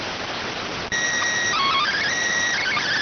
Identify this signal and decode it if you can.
SSTV (slow-scan television), PD 290.